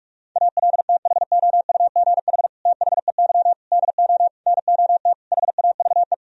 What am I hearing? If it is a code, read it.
Morse code, heard as a keyed tone, about 46 wpm, 700 Hz. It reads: ALTHOUGH THEY DO NOT HAVE
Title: Sentences from Top 1000 Words - Review Part 3 - 45wpm